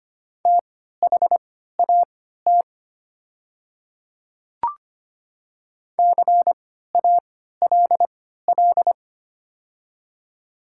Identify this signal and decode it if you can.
Morse code, heard as a keyed tone, about 25 wpm, spaced as for 14 wpm, 700 Hz. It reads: THAT CALL
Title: 3x Character Spacing - Top 100 Words - 25wpm